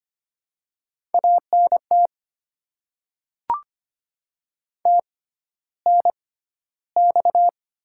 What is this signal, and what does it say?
Morse code, heard as a keyed tone, about 25 wpm, spaced as for 13 wpm, 700 Hz. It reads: ANT TNX
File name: Top 20 QSO Elements - 6x Character Spacing - 25wpm